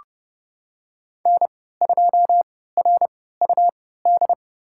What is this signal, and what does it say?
Morse code, heard as a keyed tone, about 30 wpm, spaced as for 17 wpm, 700 Hz. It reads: N2RUD
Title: USA Call Signs - 5 Characters - 3x Character Spacing - 30wpm